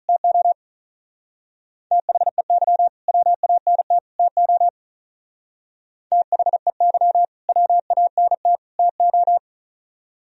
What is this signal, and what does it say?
Morse code, heard as a keyed tone, about 44 wpm, 700 Hz. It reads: TO THEY WANT TO THEY WANT TO